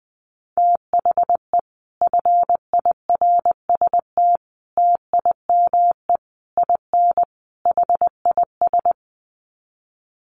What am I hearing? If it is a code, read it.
Morse code, heard as a keyed tone, about 20 wpm, 700 Hz. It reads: THE FIRST TIME IN HIS
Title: Sets of 5 Words Encore2 20wpm